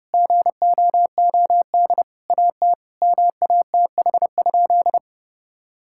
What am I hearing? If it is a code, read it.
Morse code, heard as a keyed tone, about 30 wpm, 700 Hz. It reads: GOOD AT MATH?
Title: Sentences from Top 1000 Words - Review Part 6 - 30wpm